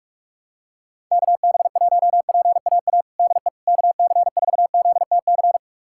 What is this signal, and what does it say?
Morse code, heard as a keyed tone, about 42 wpm, 700 Hz. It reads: KB1PAA DE KC4ZTC
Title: USA Callsign DE Callsign - 6 Characters - 45wpm